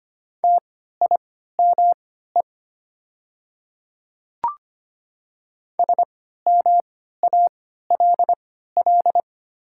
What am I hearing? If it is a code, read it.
Morse code, heard as a keyed tone, about 25 wpm, spaced as for 14 wpm, 690 Hz. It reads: TIME SMALL
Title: 3x Character Spacing - Top 200 Words - 25wpm